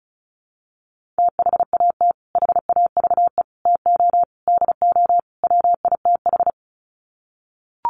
Morse code, heard as a keyed tone, about 34 wpm, 700 Hz. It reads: THAT HAVE TO DO WITH
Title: Sets of 5 Words Encore2 35wpm